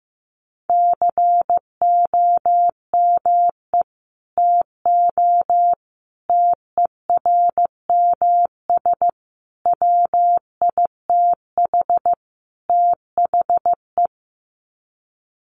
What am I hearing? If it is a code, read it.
Morse code, heard as a keyed tone, about 15 wpm, 700 Hz. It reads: COME TO TERMS WITH THE
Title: Sets of 5 Words Encore2 15wpm